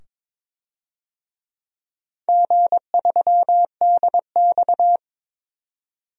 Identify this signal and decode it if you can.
Morse code, heard as a keyed tone, about 22 wpm, 690 Hz. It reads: G3DX
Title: DXCC-100 Call Signs - 4 Characters - 22wpm